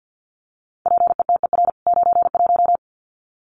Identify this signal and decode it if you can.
Morse code, heard as a keyed tone, about 50 wpm, 700 Hz. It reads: PETER 91
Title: CWT Exchange 50wpm